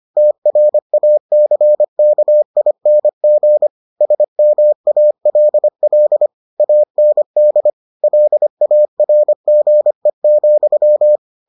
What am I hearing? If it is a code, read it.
Morse code, heard as a keyed tone, about 25 wpm, 600 Hz. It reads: TRACKING SMALL AND LARGE,